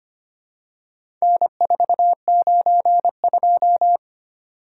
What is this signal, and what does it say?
Morse code, heard as a keyed tone, about 25 wpm, 700 Hz. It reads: N492
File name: practice / 25wpm / Four Letter-Number Combinations - 25wpm